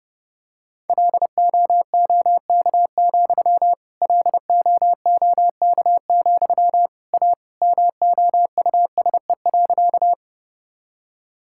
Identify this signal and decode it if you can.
Morse code, heard as a keyed tone, about 30 wpm, 700 Hz. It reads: LOOK, LOOK, A MOUSE.
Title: Sentences from Top 1000 Words - Review Part 5 - 30wpm